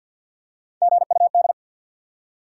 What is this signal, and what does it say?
Morse code, heard as a keyed tone, about 53 wpm, 700 Hz. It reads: GUD
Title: Top 180 QSO Elements 50wpm